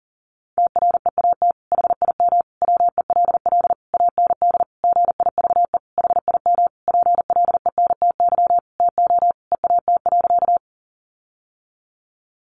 Morse code, heard as a keyed tone, about 40 wpm, 700 Hz. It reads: TREAT HIM WELL AND GIVE HIM PLENTY TO EAT.
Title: Sentences from Top 1300 Words 40wpm